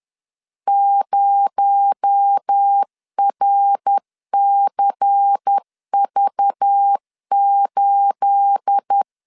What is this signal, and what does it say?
Morse code, heard as a keyed tone, about 10 wpm, 800 Hz. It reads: 0RCV8
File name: LEZIONI / 10 SIGNS / 10A SLASH 10wpm 2808